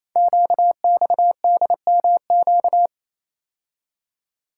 Morse code, heard as a keyed tone, about 29 wpm, 700 Hz. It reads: QXDMQ
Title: Code Groups - 5 Letters - 28wpm